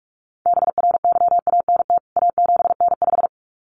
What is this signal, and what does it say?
Morse code, heard as a keyed tone, about 44 wpm, 700 Hz. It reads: BRYANT A7N5
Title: CWT Exchange 45wpm